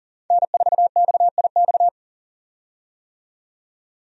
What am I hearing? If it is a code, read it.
Morse code, heard as a keyed tone, about 40 wpm, 700 Hz. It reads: N4XIX
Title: USA Call Signs - 5 Characters - 40wpm